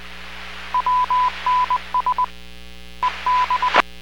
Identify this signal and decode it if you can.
Morse code, heard as a keyed tone, about 20 wpm, 1000 Hz. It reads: WNS EB